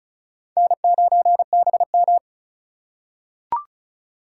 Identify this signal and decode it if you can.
Morse code, heard as a keyed tone, about 36 wpm, 700 Hz. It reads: N9BM
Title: USA Call Signs - 4 Characters - 2x Character Spacing - 35wpm